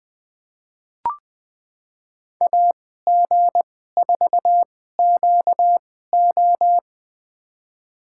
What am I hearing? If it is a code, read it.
Morse code, heard as a keyed tone, about 20 wpm, spaced as for 14 wpm, 700 Hz. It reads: AG4QO
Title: USA Call Signs - 5 Characters - 2x Character Spacing - 20wpm